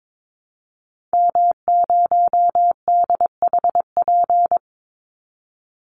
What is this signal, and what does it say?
Morse code, heard as a keyed tone, about 22 wpm, 700 Hz. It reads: M0DHP
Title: Call Signs - 22wpm